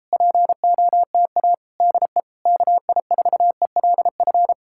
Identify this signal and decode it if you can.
Morse code, heard as a keyed tone, about 37 wpm, 700 Hz. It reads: POTA DE KI4ELF